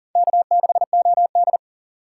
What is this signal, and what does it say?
Morse code, heard as a keyed tone, about 40 wpm, 700 Hz. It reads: K6OD
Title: USA Call Signs - 4 Characters - 6x Character Spacing - 40wpm